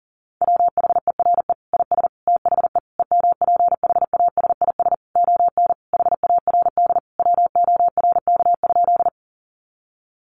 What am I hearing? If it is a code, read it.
Morse code, heard as a keyed tone, about 40 wpm, 700 Hz. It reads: WHERE IS THE EMPHASIS ON HARD WORK?
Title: Sentences from Top 1400 Words 40wpm